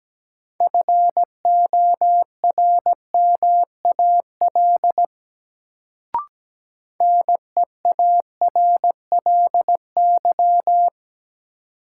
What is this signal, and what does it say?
Morse code, heard as a keyed tone, about 17 wpm, 700 Hz. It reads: FORMAL NEARLY
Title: Sentences from Top 1300 Words - New Word Drill - 17wpm